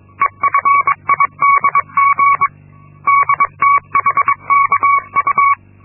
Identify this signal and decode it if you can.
Morse code, heard as a keyed tone, about 22 wpm, 1120 Hz. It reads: EFIDG DTHKU
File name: Linksmakalnis-2